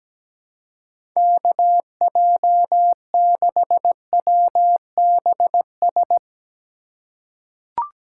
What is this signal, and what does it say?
Morse code, heard as a keyed tone, about 17 wpm, 700 Hz. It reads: KJ6WBS
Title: USA Call Signs - 4x Character Spacing - 17wpm